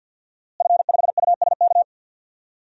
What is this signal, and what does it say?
Morse code, heard as a keyed tone, about 49 wpm, 700 Hz. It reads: F5USK
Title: DXCC-100 Call Signs - 5 Characters - Repeated 2x - 50wpm